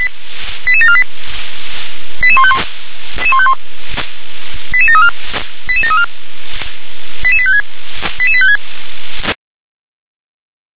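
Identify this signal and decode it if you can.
Five-tone selective call (ZVEI), heard as quick runs of five tones, tones of about 80 ms. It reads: 88748 88151 88734 88734 88856 88856
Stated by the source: ZVEI five-tone selective call 88749 received on an SDR.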